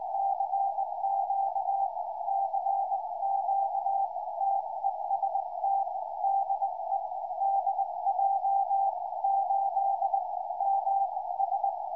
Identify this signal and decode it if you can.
Morse code, heard as a keyed tone, about 19 wpm, spaced as for 16 wpm, 790 Hz, with radio noise. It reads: CQ CQ TEST DE DJ8MS
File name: dj8ms 2m